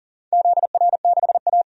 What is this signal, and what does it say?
Morse code, heard as a keyed tone, about 39 wpm, 700 Hz. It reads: ZR6A
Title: DXCC-100 Call Signs - 4 Characters - 40wpm